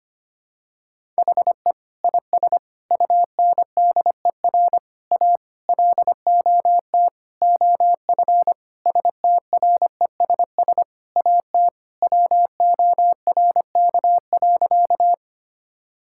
Morse code, heard as a keyed tone, about 25 wpm, 700 Hz. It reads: HE IS UNDER A LOT OF STRESS AT WORK.
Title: Sentences from Top 1000 Words - Review Part 6 - 25wpm